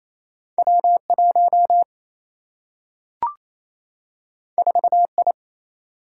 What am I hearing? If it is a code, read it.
Morse code, heard as a keyed tone, about 28 wpm, 700 Hz. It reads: W 1 4 I
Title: Single-Letter Single-Number 28wpm